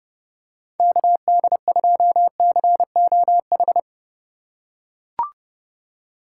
Morse code, heard as a keyed tone, about 32 wpm, 700 Hz. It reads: KD2COH
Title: USA Call Signs - 6 Characters - 3x Character Spacing - 30wpm